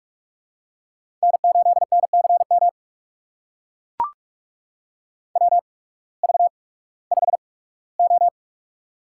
Morse code, heard as a keyed tone, about 44 wpm, spaced as for 21 wpm, 700 Hz. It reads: N8NCM WV5O
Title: USA Call Signs - 8x Character Spacing - 45wpm